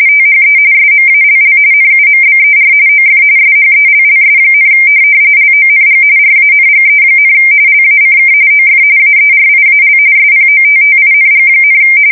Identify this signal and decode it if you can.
RTTY, about 45.45 baud. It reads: THE SMALL NUMBER OF SHELLS, LONG RANGE, LACK OF AERIAL SPOTTING,
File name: fsk170 / USS-New-Jersey